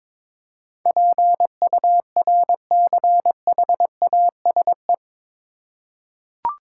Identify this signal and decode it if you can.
Morse code, heard as a keyed tone, about 22 wpm, 700 Hz. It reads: PURCHASE
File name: practice / 22wpm / Eight Letter Words - 22wpm